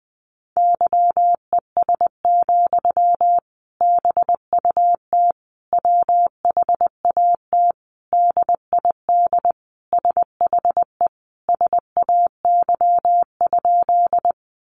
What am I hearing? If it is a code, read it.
Morse code, heard as a keyed tone, about 19 wpm, 700 Hz. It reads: YES, BUT WHAT DID SHE SAY?
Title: Sentences from Top 500 Words 20wpm